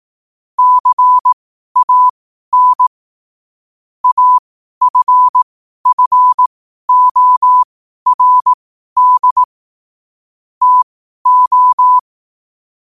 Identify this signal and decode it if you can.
Morse code, heard as a keyed tone, about 18 wpm, spaced as for 12 wpm, 1000 Hz. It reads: CAN AFFORD TO
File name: qotdfiles / 12wpm